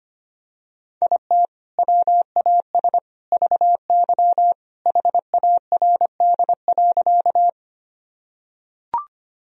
Morse code, heard as a keyed tone, about 25 wpm, 700 Hz. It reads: IT WAS VY HARD.
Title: Sentences from Top 300 Words and Abbreviations 25wpm